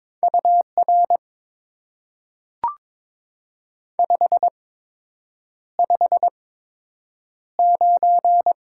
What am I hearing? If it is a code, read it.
Morse code, heard as a keyed tone, about 22 wpm, spaced as for 8 wpm, 700 Hz. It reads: UR 559
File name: Top 90 QSO Elements - 8x Character Spacing - 22wpm